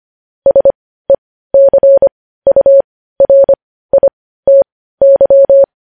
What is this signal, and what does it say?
Morse code, heard as a keyed tone, about 25 wpm, spaced as for 15 wpm, 550 Hz. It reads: SECURITY
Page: news headlines in morse code at 15 wpm